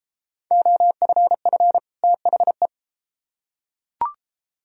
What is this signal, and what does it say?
Morse code, heard as a keyed tone, about 29 wpm, 700 Hz. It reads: OFF THE